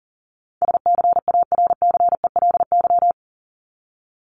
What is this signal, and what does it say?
Morse code, heard as a keyed tone, about 40 wpm, 700 Hz. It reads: SCARCELY
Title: LY Suffix - Top 100 Words with Base Words 40wpm